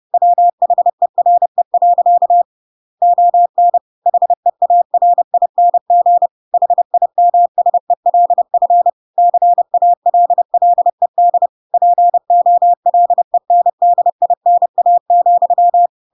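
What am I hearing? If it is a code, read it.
Morse code, heard as a keyed tone, about 30 wpm, 700 Hz. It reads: WHERE. ON HEARING HIMSELF CALLED POLENDINA,
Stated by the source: Chapter 2 of The Adventures of Pinocchio read as a book in Morse code (CW) at 30 wpm, by ik1hge.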